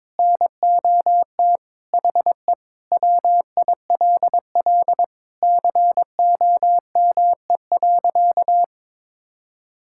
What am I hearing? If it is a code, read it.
Morse code, heard as a keyed tone, about 23 wpm, 700 Hz. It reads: NOT HE WILL COME.
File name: practice / 22wpm / Sentences with 3-Word Phrases - 22wpm